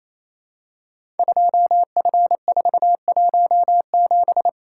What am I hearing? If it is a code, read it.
Morse code, heard as a keyed tone, about 28 wpm, 700 Hz. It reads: 2F417
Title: Code Groups - 5 Letter:Numbers - 28wpm